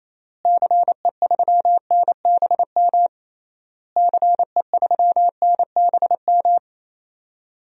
Morse code, heard as a keyed tone, about 28 wpm, 700 Hz. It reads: CE3NBM CE3NBM
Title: DXCC-100 Call Signs - 6 Characters - Repeated 3x - 28wpm